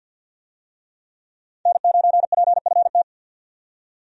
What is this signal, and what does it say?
Morse code, heard as a keyed tone, about 52 wpm, 700 Hz. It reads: N9PFT
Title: USA Call Signs - 5 Characters - 50wpm